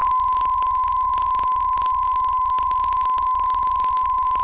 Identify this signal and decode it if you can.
PSK63, about 62.5 baud.
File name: UVB33-2016-06-11-1500utc-12025khz-byW2IBC